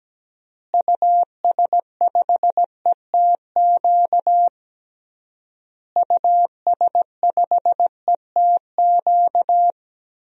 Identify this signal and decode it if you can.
Morse code, heard as a keyed tone, about 17 wpm, 700 Hz. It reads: US5ETQ US5ETQ
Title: DXCC-100 Call Signs - 6 Characters - Repeated 3x - 17wpm